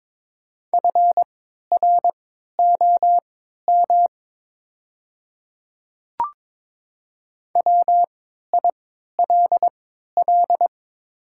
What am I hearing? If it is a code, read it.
Morse code, heard as a keyed tone, about 22 wpm, spaced as for 12 wpm, 700 Hz. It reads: FROM WILL
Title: 3x Character Spacing - Top 100 Words - 22wpm